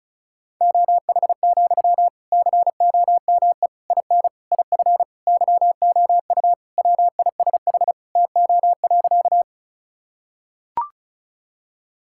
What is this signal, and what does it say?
Morse code, heard as a keyed tone, about 35 wpm, 700 Hz. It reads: OH, COME IN IF YOU WISH TO.
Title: Sentences from Top 1600 Words 35wpm